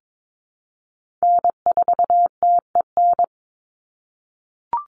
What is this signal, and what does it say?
Morse code, heard as a keyed tone, about 22 wpm, 700 Hz. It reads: N4TEN
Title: USA Call Signs 22wpm